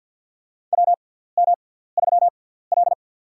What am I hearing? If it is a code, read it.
Morse code, heard as a keyed tone, about 48 wpm, spaced as for 17 wpm, 700 Hz. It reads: WM3L
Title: USA Call Signs - 4 Characters - 6x Character Spacing - 50wpm